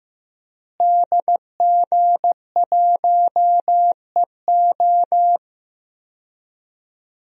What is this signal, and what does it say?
Morse code, heard as a keyed tone, about 15 wpm, 700 Hz. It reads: DG1EO
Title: DXCC-100 Call Signs - 5 Characters - Repeated 3x - 15wpm